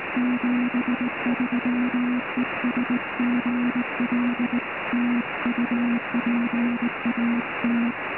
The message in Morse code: 73 ES GL TU PAT